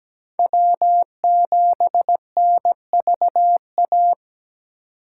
Morse code, heard as a keyed tone, about 17 wpm, 700 Hz. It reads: W7NVA
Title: USA Call Signs - 5 Characters - 8x Character Spacing - 17wpm